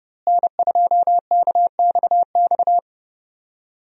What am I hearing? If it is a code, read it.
Morse code, heard as a keyed tone, about 30 wpm, 700 Hz. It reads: N2KXX
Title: USA Call Signs - 5 Characters - Repeated 3x - 30wpm